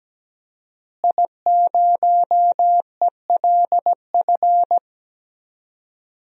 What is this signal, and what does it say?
Morse code, heard as a keyed tone, about 17 wpm, 700 Hz. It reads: I0ELF
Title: DXCC-100 Call Signs - 5 Characters - Repeated 2x - 17wpm